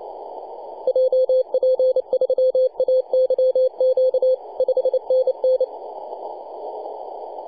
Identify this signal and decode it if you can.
Morse code, heard as a keyed tone, about 29 wpm, 520 Hz, with radio noise. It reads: JP3AYQ 5NN
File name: posts / 2015-june-21-0848z-yb4ir-9-14-010mhz-cw